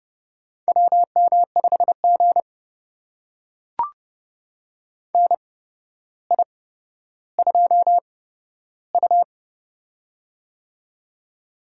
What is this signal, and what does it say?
Morse code, heard as a keyed tone, about 29 wpm, spaced as for 13 wpm, 700 Hz. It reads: WM5G NI2U
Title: USA Call Signs - 4 Characters - 8x Character Spacing - 30wpm